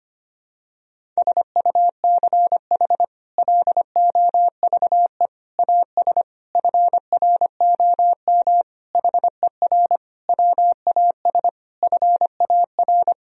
Morse code, heard as a keyed tone, about 25 wpm, 700 Hz. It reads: SUCH LOVE AS FROM HER WAS FAR
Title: Sentences from Top 1000 Words - Review Part 1 - 25wpm